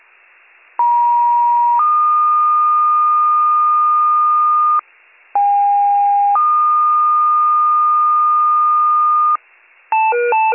Two-tone page, heard as a one-second tone, then a three-second tone, about 950 then 1220 Hz, about 810 then 1220 Hz. It is heard from 0.8 to 9.4 s.